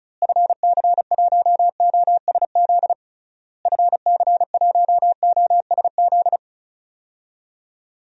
Morse code, heard as a keyed tone, about 34 wpm, 700 Hz. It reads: FC1OSZ FC1OSZ
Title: DXCC-100 Call Signs - 6 Characters - Repeated 3x - 35wpm